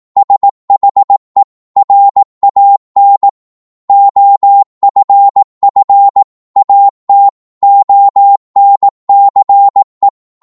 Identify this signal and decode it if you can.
Morse code, heard as a keyed tone, about 18 wpm, 800 Hz. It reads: SHE RAN OFF AT ONCE